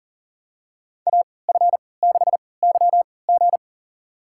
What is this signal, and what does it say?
Morse code, heard as a keyed tone, about 37 wpm, spaced as for 22 wpm, 700 Hz. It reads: AF6YG
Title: USA Call Signs - 5 Characters - 3x Character Spacing - 40wpm